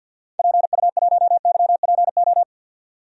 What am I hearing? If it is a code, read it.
Morse code, heard as a keyed tone, about 49 wpm, 700 Hz. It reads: PU1YPO
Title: DXCC-100 Call Signs - 6 Characters - 50wpm